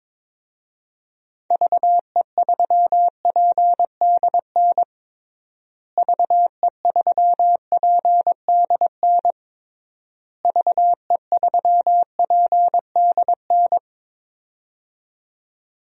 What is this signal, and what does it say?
Morse code, heard as a keyed tone, about 22 wpm, 700 Hz. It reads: VE3PDN VE3PDN VE3PDN
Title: DXCC-100 Call Signs - 6 Characters - Repeated 3x - 22wpm